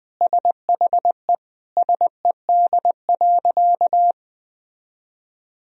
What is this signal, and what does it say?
Morse code, heard as a keyed tone, about 20 wpm, 700 Hz. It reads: SHE SED.
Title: Sentences from Top 400 Words and Abbreviations 20wpm